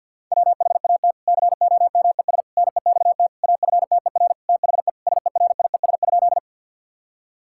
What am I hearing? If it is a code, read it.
Morse code, heard as a keyed tone, about 50 wpm, 700 Hz. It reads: WHAT COMES NEXT AFTER THE SERIES?
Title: Sentences from Top 1000 Words - Review Part 5 - 50wpm